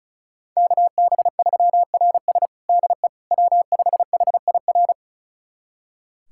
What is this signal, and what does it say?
Morse code, heard as a keyed tone, about 34 wpm, 700 Hz. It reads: KB3RS DE W5HIR
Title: USA Callsign DE Callsign 35wpm